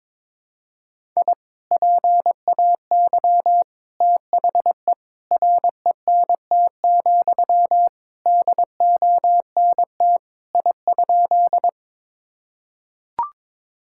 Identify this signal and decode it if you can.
Morse code, heard as a keyed tone, about 22 wpm, 700 Hz. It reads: I PAY THE RENT, DONT I?
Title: Sentences from Top 1000 Words - Review Part 5 - 22wpm